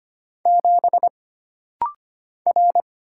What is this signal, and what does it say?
Morse code, heard as a keyed tone, about 25 wpm, 700 Hz. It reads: 7 R
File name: Single Character (Letter:Number:Punctuation:Prosign) - Rapid-Fire - 25wpm